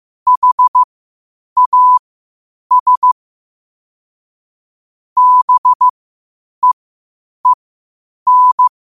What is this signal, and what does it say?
Morse code, heard as a keyed tone, about 15 wpm, spaced as for 8 wpm, 1000 Hz. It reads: HAS BEEN
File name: qotdfiles / 8wpm